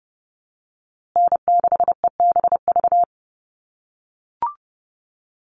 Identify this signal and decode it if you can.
Morse code, heard as a keyed tone, about 30 wpm, 700 Hz. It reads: N6EBV
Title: USA Call Signs Repeated 3x 30wpm